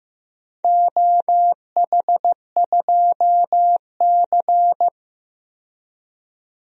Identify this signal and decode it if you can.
Morse code, heard as a keyed tone, about 15 wpm, 700 Hz. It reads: OH2C
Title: DXCC-100 Call Signs - 4 Characters - 15wpm